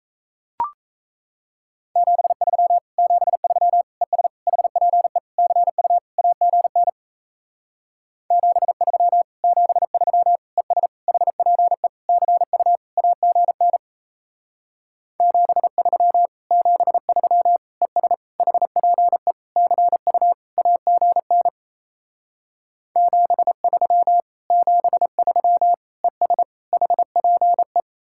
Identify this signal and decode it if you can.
Morse code, heard as a keyed tone, about 37 wpm, 700 Hz. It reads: 73 73 ES HPE CU AGN 73 73 ES HPE CU AGN 73 73 ES HPE CU AGN 73 73 ES HPE